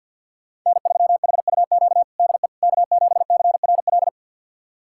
Morse code, heard as a keyed tone, about 48 wpm, 700 Hz. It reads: N3HUQ DE K7CRL
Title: USA Callsign DE Callsign - 5 Characters - 50wpm